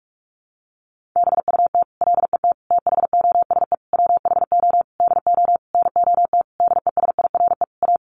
Morse code, heard as a keyed tone, about 44 wpm, 700 Hz. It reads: BUT LET THOSE WHO DO NOT DESIRE A
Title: Sentences from Top 1500 Words 45wpm